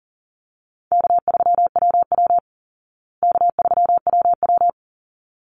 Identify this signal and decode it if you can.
Morse code, heard as a keyed tone, about 40 wpm, 700 Hz. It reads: K3WW K3WW
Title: Easy Call Signs Repeated 2x - 40wpm